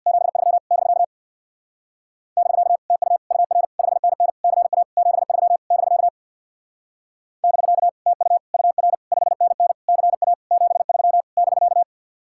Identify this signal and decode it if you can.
Morse code, heard as a keyed tone, about 66 wpm, 700 Hz. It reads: 73 <BK> <BK> TU UR 5NN CA 73 <BK> <BK> TU UR 5NN CA 73 <BK>